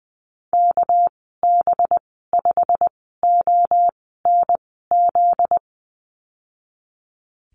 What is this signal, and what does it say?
Morse code, heard as a keyed tone, about 20 wpm, spaced as for 15 wpm, 700 Hz. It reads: KB5ONZ
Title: 245 Lesson 39g - Letter z - Call Signs 20wpm